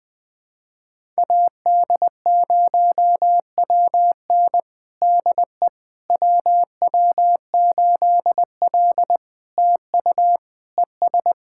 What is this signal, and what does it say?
Morse code, heard as a keyed tone, about 20 wpm, 700 Hz. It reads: AD0WN DE WW8L TU ES
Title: POTA QSO Practice - 20wpm